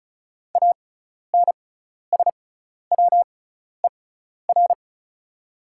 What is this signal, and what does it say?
Morse code, heard as a keyed tone, about 36 wpm, spaced as for 12 wpm, 690 Hz. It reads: ANSWER
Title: 6x Character Spacing - Top 300 Words - 35wpm